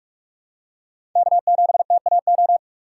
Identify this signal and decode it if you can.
Morse code, heard as a keyed tone, about 46 wpm, 700 Hz. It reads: K7TAO
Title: USA Call Signs - 5 Characters - 2x Character Spacing - 45wpm